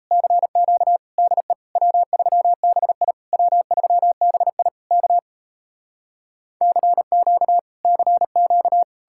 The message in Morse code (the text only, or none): CQ DE W3BI W3BI K CQ CQ